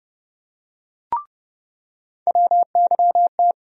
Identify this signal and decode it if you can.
Morse code, heard as a keyed tone, about 30 wpm, 700 Hz. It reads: WYT
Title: USA Call Sign Postfixes - 3 Letters 30wpm